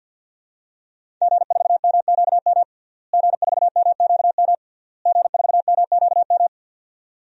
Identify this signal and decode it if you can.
Morse code, heard as a keyed tone, about 52 wpm, 700 Hz. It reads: G4MQM G4MQM G4MQM
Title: DXCC-100 Call Signs - 5 Characters - Repeated 3x - 50wpm